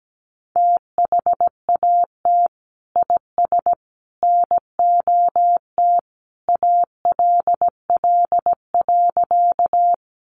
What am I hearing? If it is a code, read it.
Morse code, heard as a keyed tone, about 17 wpm, 700 Hz. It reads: THAT IS NOT ALL.
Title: Sentences from Top 600 Words 17wpm